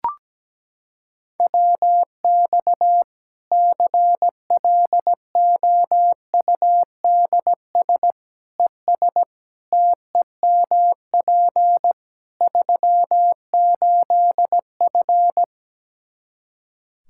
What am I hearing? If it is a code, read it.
Morse code, heard as a keyed tone, about 17 wpm, 700 Hz. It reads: WX CLOUDS ES TEMP 38F
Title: Practice QSO - Exchange 2 - Info - 17wpm